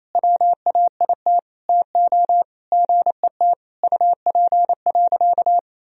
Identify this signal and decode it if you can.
Morse code, heard as a keyed tone, about 28 wpm, 700 Hz. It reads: WAIT TO GET UP.
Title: Sentences from Top 500 Words and Abbreviations 28wpm